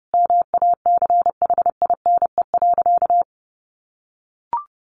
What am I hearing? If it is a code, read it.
Morse code, heard as a keyed tone, about 30 wpm, 700 Hz. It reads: MACHINE.
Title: A Sound Of Thunder 30wpm